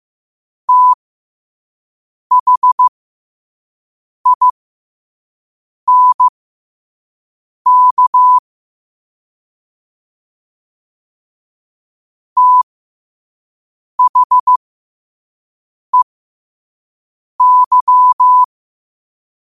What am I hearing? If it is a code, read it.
Morse code, heard as a keyed tone, about 15 wpm, spaced as for 5 wpm, 1000 Hz. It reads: THINK THEY
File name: qotdfiles / 5wpm